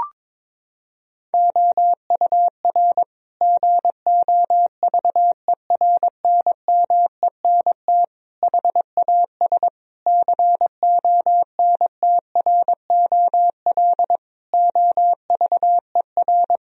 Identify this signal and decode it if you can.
Morse code, heard as a keyed tone, about 22 wpm, 700 Hz. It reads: OUR GOVERNMENT HAS CONTROL OVER
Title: Sentences from Top 1000 Words - Review Part 4 - 22wpm